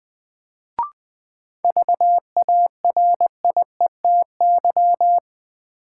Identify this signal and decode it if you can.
Morse code, heard as a keyed tone, about 20 wpm, 700 Hz. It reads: VARIETY
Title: Sets of 3 Words - New Word Drill - 20wpm